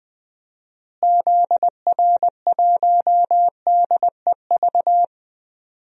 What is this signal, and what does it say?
Morse code, heard as a keyed tone, about 21 wpm, 700 Hz. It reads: ZR1DEV
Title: DXCC-100 Call Signs - 6 Characters - Repeated 3x - 20wpm